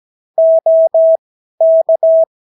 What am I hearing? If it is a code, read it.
Morse code, heard as a keyed tone, about 17 wpm, spaced as for 12 wpm, 650 Hz. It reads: OK
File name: CW QSO libroCW4 18-8ppm